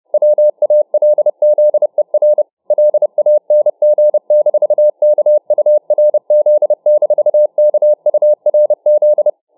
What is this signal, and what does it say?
Morse code, heard as a keyed tone, about 31 wpm, 600 Hz. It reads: WALZER LANG-KURZ-KURZ